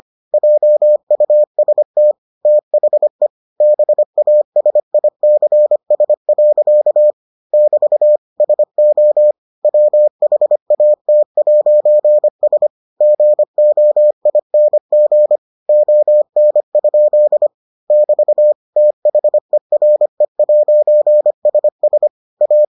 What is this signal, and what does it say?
Morse code, heard as a keyed tone, about 25 wpm, 600 Hz. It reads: JUST THE BASICS. <BT> SO WHAT'S GOING ON? <BT> THERE'SS A